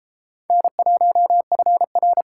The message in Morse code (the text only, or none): N1FR